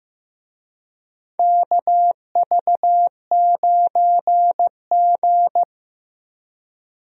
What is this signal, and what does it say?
Morse code, heard as a keyed tone, about 15 wpm, 700 Hz. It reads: KV9G
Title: USA Call Signs - 4 Characters - Repeated 3x - 15wpm